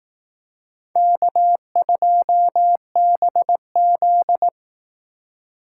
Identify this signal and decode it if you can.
Morse code, heard as a keyed tone, about 18 wpm, 700 Hz. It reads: K2BZ
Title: Speed Racing - USA Call Signs 15wpm